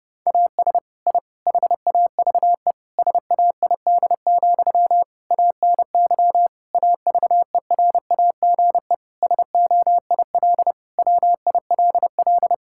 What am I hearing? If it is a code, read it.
Morse code, heard as a keyed tone, about 30 wpm, 700 Hz. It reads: AS I HAVE SAID, ANY AVERAGE SOIL WILL
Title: Sentences from Top 1000 Words - Review Part 6 - 30wpm